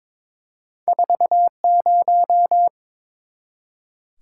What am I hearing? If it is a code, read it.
Morse code, heard as a keyed tone, about 22 wpm, 700 Hz. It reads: 40
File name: practice / 22wpm / Two Letter-Number Combinations - 22wpm